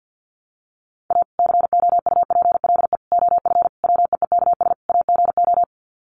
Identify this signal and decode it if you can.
Morse code, heard as a keyed tone, about 50 wpm, 700 Hz. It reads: A COUPLE OF WEEKS AGO
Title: Sets of 5 Words 50wpm